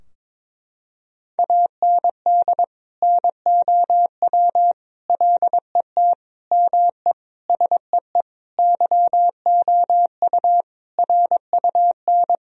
Morse code, heard as a keyed tone, about 22 wpm, 700 Hz. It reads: AND NOW LET ME SEE YOU RUN
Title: Sentences from Top 1000 Words - Review Part 2 - 22wpm